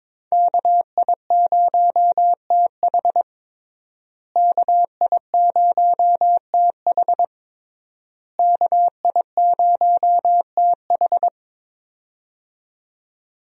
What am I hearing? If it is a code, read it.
Morse code, heard as a keyed tone, about 22 wpm, 700 Hz. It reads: KI0TH KI0TH KI0TH
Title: USA Call Signs - 5 Characters - Repeated 3x - 22wpm